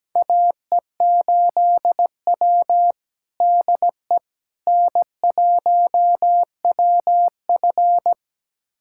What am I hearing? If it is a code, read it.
Morse code, heard as a keyed tone, about 17 wpm, 700 Hz. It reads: AE8W DE N1WF
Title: USA Callsign DE Callsign - 4 Characters - 17wpm